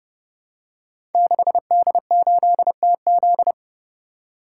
Morse code, heard as a keyed tone, about 28 wpm, 700 Hz. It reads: 6D8TZ
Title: Code Groups - 5 Letter:Numbers - 30wpm